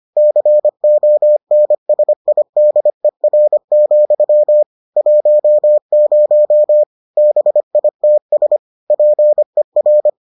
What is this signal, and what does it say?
Morse code, heard as a keyed tone, about 25 wpm, 600 Hz. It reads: CONSIDER, 10 BITS PER